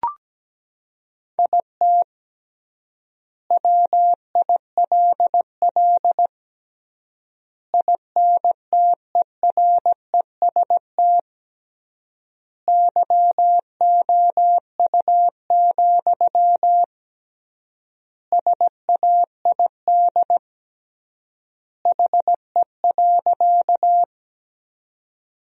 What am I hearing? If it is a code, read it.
Morse code, heard as a keyed tone, about 17 wpm, 700 Hz. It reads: IT WILL INTEREST YOU, SAID HE.
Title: 3x Word-Spacing - Sentences from Top 400 Words - 17wpm